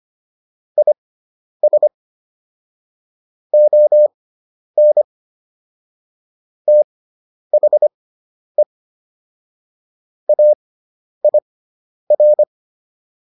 Morse code, heard as a keyed tone, about 25 wpm, spaced as for 10 wpm, 600 Hz. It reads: IS ON THE AIR